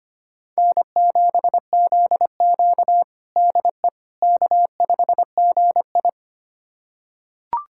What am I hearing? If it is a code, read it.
Morse code, heard as a keyed tone, about 25 wpm, 700 Hz. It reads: N7ZQ DE K5GI
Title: USA Callsign DE Callsign - 4 Characters - 25wpm